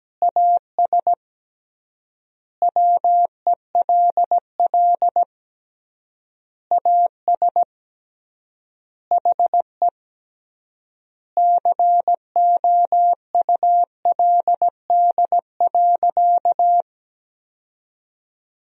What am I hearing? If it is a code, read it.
Morse code, heard as a keyed tone, about 17 wpm, 700 Hz. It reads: AS WELL AS HE COULD.
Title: 3x Word-Spacing - Sentences from Top 500 Words - 17wpm